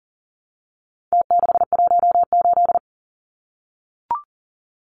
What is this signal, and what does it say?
Morse code, heard as a keyed tone, about 40 wpm, 700 Hz. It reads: T618
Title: Numbers - Contest Style - 4-Digit 40wpm